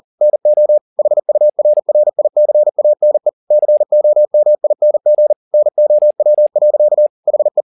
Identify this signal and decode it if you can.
Morse code, heard as a keyed tone, about 40 wpm, 600 Hz. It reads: NO HURRICANE COMING NOW. HE